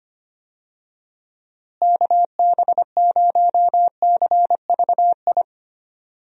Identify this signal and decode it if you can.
Morse code, heard as a keyed tone, about 25 wpm, 700 Hz. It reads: KB0CVI
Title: USA Call Signs - 6 Characters - Repeated 2x - 25wpm